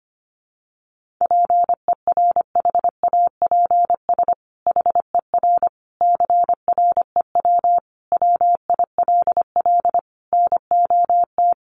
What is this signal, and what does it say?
Morse code, heard as a keyed tone, about 25 wpm, 700 Hz. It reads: PERHAPS HER CREW WILL NOT
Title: Sentences from Top 1500 Words 25wpm